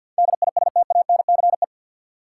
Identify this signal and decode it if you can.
Morse code, heard as a keyed tone, about 51 wpm, 700 Hz. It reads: DISTANCE
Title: Sentences from Top 1200 Words - New Word Drill Alt - 50wpm